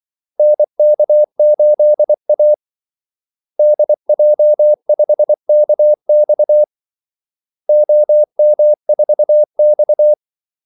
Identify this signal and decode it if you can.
Morse code, heard as a keyed tone, about 24 wpm, 600 Hz. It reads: NK8A DJ5KX OM4X
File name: Callsigns-24wpm